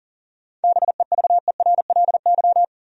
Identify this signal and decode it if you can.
Morse code, heard as a keyed tone, about 40 wpm, 700 Hz. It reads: BEVERLY
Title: USA Common Names - 40wpm